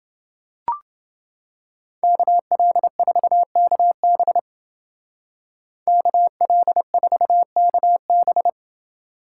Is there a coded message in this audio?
Yes: Morse code from 2.0 to 8.5 s, heard as a keyed tone, about 30 wpm, 700 Hz. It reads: KL4KB KL4KB